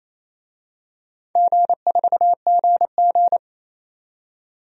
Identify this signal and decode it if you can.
Morse code, heard as a keyed tone, about 28 wpm, 700 Hz. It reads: G4GG
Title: DXCC-100 Call Signs - 4 Characters - 28wpm